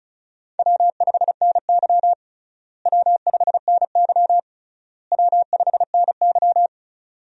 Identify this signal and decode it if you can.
Morse code, heard as a keyed tone, about 36 wpm, 700 Hz. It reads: W5NY W5NY W5NY
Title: USA Call Signs - 4 Characters - Repeated 3x - 35wpm